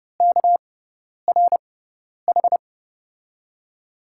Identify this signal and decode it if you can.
Morse code, heard as a keyed tone, about 30 wpm, spaced as for 10 wpm, 700 Hz. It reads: KRH
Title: USA Call Signs - 5 Characters - 6x Character Spacing - 30wpm